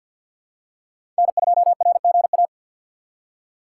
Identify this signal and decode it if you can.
Morse code, heard as a keyed tone, about 47 wpm, 700 Hz. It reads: N2RGA
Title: USA Call Signs - 4x Character Spacing - 50wpm